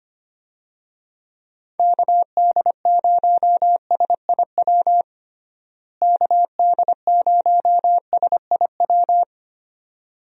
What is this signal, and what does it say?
Morse code, heard as a keyed tone, about 25 wpm, 700 Hz. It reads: KD0SIW KD0SIW
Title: USA Call Signs - 6 Characters - Repeated 2x - 25wpm